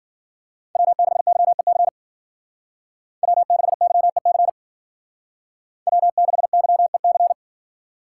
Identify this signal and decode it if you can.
Morse code, heard as a keyed tone, about 50 wpm, 700 Hz. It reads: W6YEC W6YEC W6YEC